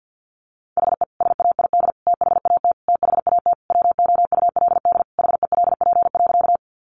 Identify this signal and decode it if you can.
Morse code, heard as a keyed tone, about 48 wpm, 700 Hz. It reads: HE SAID THAT THAT WOULD HELP.
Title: Sentences from Top 200 Words 50wpm